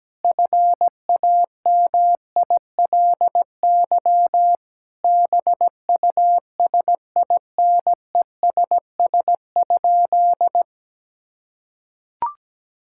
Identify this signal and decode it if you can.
Morse code, heard as a keyed tone, about 17 wpm, 700 Hz. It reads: FAMILY BUSINESS?
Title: Sentences from Top 1000 Words - Review Part 6 - 17wpm